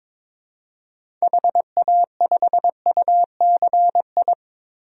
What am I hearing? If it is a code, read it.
Morse code, heard as a keyed tone, about 22 wpm, 700 Hz. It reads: HA5UCI
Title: DXCC-100 Call Signs - 6 Characters - 22wpm